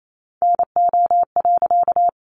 Morse code, heard as a keyed tone, about 28 wpm, 700 Hz. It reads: NO.
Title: A Sound Of Thunder 28wpm